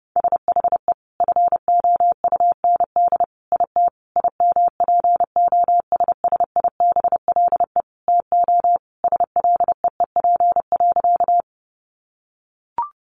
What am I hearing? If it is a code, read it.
Morse code, heard as a keyed tone, about 30 wpm, 700 Hz. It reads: SHE FOUND IT IMPOSSIBLE TO SLEEP.
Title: Sentences from Top 1100 Words 30wpm